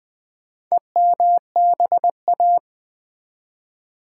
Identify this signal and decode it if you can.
Morse code, heard as a keyed tone, about 20 wpm, 700 Hz. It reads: EMBA
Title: Build A Word - 4 Letters At A Time - 11-Letter Words - 20wpm